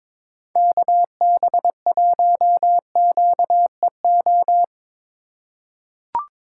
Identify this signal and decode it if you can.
Morse code, heard as a keyed tone, about 22 wpm, 700 Hz. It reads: KB1QEO
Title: USA Call Signs - 6 Characters - 6x Character Spacing - 22wpm